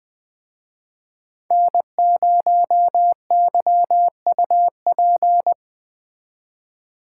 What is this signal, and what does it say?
Morse code, heard as a keyed tone, about 24 wpm, 700 Hz. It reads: N0YUP
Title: Speed Racing - USA 1x3 Call Signs 20wpm